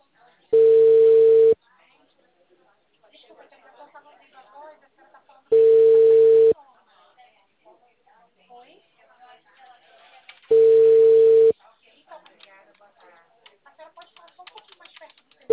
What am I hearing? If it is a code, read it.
Ringback tone (China, 450 Hz), with speech.